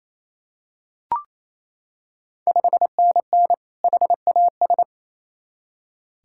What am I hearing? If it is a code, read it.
Morse code, heard as a keyed tone, about 28 wpm, 700 Hz. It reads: 5NN HAS
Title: Ontario QSO Party Exchange 28wpm